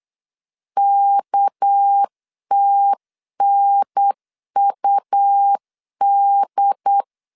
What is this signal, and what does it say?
Morse code, heard as a keyed tone, about 8 wpm, 800 Hz. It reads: KTNUD
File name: CW / DET / 1 9 1500C S A 8wpm 5637